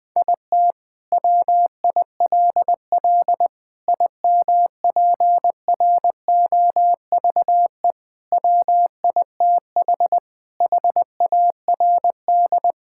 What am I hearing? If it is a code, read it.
Morse code, heard as a keyed tone, about 20 wpm, 700 Hz. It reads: IT WILL IMPROVE WITH HARD
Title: Sentences from Top 1000 Words - Review Part 5 - 20wpm